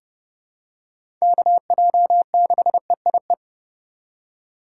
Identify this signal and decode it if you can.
Morse code, heard as a keyed tone, about 32 wpm, 700 Hz. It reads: KJ6EIE
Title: USA Call Signs - 6 Characters - 8x Character Spacing - 30wpm